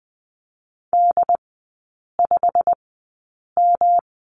Morse code, heard as a keyed tone, about 20 wpm, spaced as for 8 wpm, 700 Hz. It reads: D5M
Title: 107 Lesson 18e - Letter m - Three Characters 20wpm